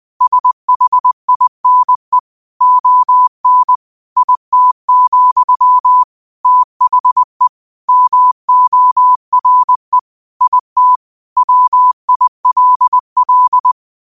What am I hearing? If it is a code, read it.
Morse code, heard as a keyed tone, about 20 wpm, 1000 Hz. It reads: SHINE ON IT, THE MORE IT WILL
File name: qotdfiles / 20wpm